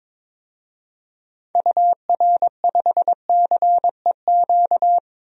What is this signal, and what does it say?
Morse code, heard as a keyed tone, about 22 wpm, 700 Hz. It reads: UR5CEQ
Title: DXCC-100 Call Signs - 6 Characters - Repeated 2x - 22wpm